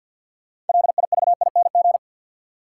Morse code, heard as a keyed tone, about 53 wpm, 700 Hz. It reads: LIVING
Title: Sentences from Top 1000 Words - Review Part 6 - New Word Drill - 50wpm